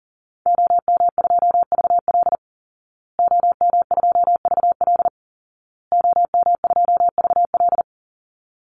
Morse code, heard as a keyed tone, about 40 wpm, 700 Hz. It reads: OM2VL OM2VL OM2VL
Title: Call Signs Repeated 2x - 40wpm